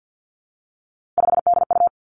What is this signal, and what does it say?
Morse code, heard as a keyed tone, about 50 wpm, 700 Hz. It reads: 5DU
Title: Three Challenging Letter-Number Combinations 50wpm